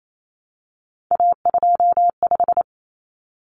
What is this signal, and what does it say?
Morse code, heard as a keyed tone, about 28 wpm, 700 Hz. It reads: A25
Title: Common Cut Numbers - 3-Digit 28wpm